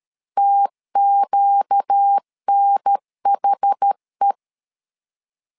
Morse code, heard as a keyed tone, about 13 wpm, 800 Hz. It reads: TQNHE
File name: LEZIONI / 4 Q 12wpm 1833